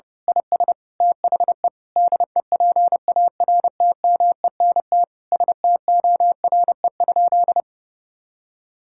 Morse code, heard as a keyed tone, about 32 wpm, 700 Hz. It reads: IS THE DEPARTMENT STORE?
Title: Sentences from Top 1000 Words - Review Part 5 - 30wpm